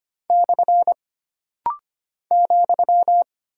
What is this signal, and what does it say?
Morse code, heard as a keyed tone, about 25 wpm, 700 Hz. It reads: / ,
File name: Single Punctuation-Prosign - Rapid-Fire - 25wpm